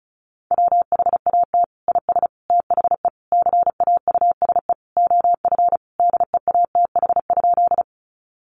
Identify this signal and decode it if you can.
Morse code, heard as a keyed tone, about 34 wpm, 700 Hz. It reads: WHAT IS THE CAUSE OF DEATH?
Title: Sentences from Top 1100 Words 35wpm